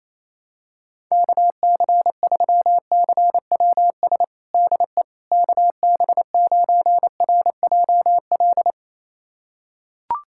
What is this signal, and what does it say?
Morse code, heard as a keyed tone, about 28 wpm, 700 Hz. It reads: KC3CWS DE KB9RJL
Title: USA Callsign DE Callsign - 6 Characters - 28wpm